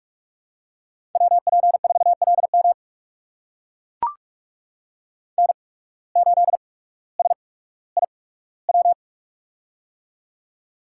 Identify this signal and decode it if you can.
Morse code, heard as a keyed tone, about 44 wpm, spaced as for 19 wpm, 700 Hz. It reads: WP4LM N8SIW
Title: USA Call Signs - 5 Characters - 8x Character Spacing - 45wpm